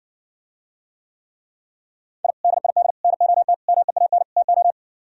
Morse code, heard as a keyed tone, about 60 wpm, 700 Hz. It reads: I DID NOT MEAN TO